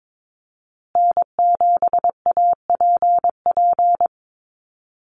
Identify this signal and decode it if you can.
Morse code, heard as a keyed tone, about 22 wpm, 700 Hz. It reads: N7APP
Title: USA Call Signs Repeated 2x 22wpm